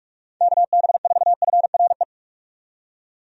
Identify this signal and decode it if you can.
Morse code, heard as a keyed tone, about 46 wpm, 700 Hz. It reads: KB4FRE
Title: USA Call Signs - 6 Characters - 45wpm